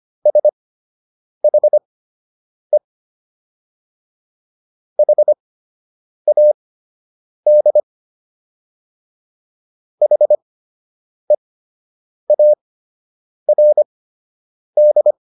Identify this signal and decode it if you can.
Morse code, heard as a keyed tone, about 25 wpm, spaced as for 8 wpm, 600 Hz. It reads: SHE HAD HEARD